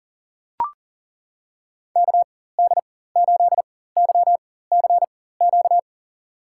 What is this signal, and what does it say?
Morse code, heard as a keyed tone, about 38 wpm, spaced as for 18 wpm, 700 Hz. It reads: KD8YCQ
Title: USA Call Signs - 6 Characters - 4x Character Spacing - 40wpm